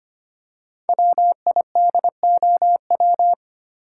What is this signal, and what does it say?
Morse code, heard as a keyed tone, about 25 wpm, 700 Hz. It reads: WIDOW
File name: 5 Letter Words - First 3 Letters Given - 25wpm